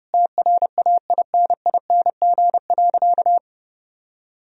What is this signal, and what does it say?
Morse code, heard as a keyed tone, about 30 wpm, 700 Hz. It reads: TRAINING.
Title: Sentences from Top 1000 Words - Review Part 4 - 30wpm